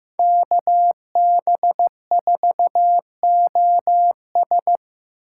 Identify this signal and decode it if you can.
Morse code, heard as a keyed tone, about 17 wpm, 700 Hz. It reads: KB4OS
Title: Speed Racing - USA 2x2 Call Signs 15wpm